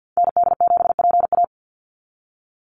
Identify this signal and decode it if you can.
Morse code, heard as a keyed tone, about 48 wpm, 700 Hz. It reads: ND7PA
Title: NA SOTA Activator Callsigns 50wpm